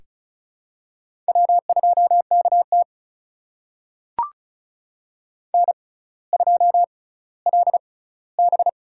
Morse code, heard as a keyed tone, about 34 wpm, spaced as for 18 wpm, 700 Hz. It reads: W2KT N2LB
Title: USA Call Signs - 6x Character Spacing - 35wpm